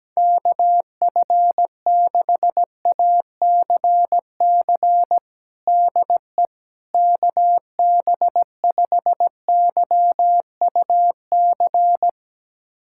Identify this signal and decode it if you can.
Morse code, heard as a keyed tone, about 17 wpm, 700 Hz. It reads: KF6ACC DE KB5YUC
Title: USA Callsign DE Callsign 17wpm